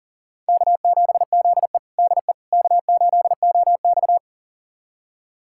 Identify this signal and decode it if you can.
Morse code, heard as a keyed tone, about 40 wpm, 700 Hz. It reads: K7ZE DE K8OX
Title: USA Callsign DE Callsign - 4 Characters - 40wpm